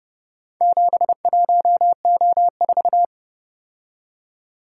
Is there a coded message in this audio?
Yes: Morse code from 0.6 to 3.0 s, heard as a keyed tone, about 29 wpm, 700 Hz. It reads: 71O4